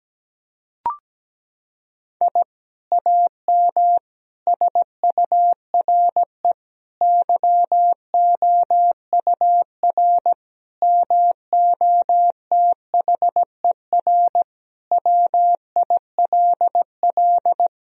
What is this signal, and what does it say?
Morse code, heard as a keyed tone, about 17 wpm, 700 Hz. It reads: I AM SURE YOUR MOTHER WILL
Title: Sentences from Top 1000 Words - Review Part 3 - 17wpm